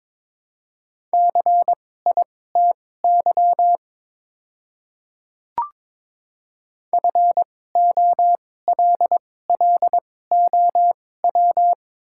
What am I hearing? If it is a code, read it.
Morse code, heard as a keyed tone, about 22 wpm, spaced as for 16 wpm, 700 Hz. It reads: CITY FOLLOW
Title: 2x Character Spacing - Top 300 Words - 22wpm